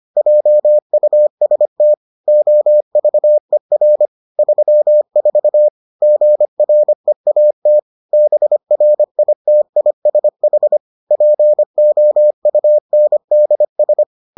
Morse code, heard as a keyed tone, about 25 wpm, 600 Hz. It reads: JUST OVER 34 GREAT BRITISH POUNDS